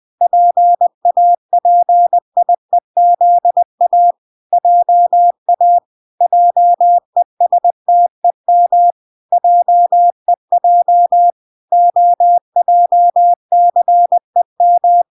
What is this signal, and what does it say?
Morse code, heard as a keyed tone, about 20 wpm, 700 Hz. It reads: PAPIEZA JA JESTEM JEJ OJCEM